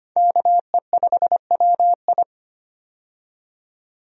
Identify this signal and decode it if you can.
Morse code, heard as a keyed tone, about 25 wpm, 700 Hz. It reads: KE5WI
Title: USA Call Signs - 5 Characters - 25wpm